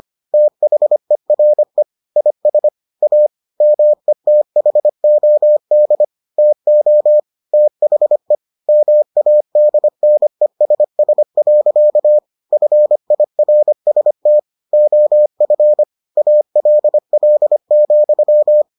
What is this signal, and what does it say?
Morse code, heard as a keyed tone, about 25 wpm, 600 Hz. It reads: THERE IS A METHOD TO THE MADNESS. FIRST OF ALL,